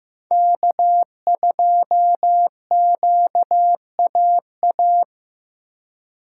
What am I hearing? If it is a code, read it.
Morse code, heard as a keyed tone, about 18 wpm, 700 Hz. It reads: K2QAA
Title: Speed Racing - USA 1x3 Call Signs 15wpm